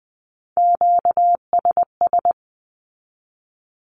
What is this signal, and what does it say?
Morse code, heard as a keyed tone, about 20 wpm, 700 Hz. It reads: QSS
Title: Q-Codes 20wpm